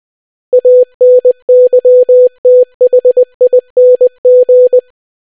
Morse code, heard as a keyed tone, about 20 wpm, 500 Hz. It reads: ANYTHING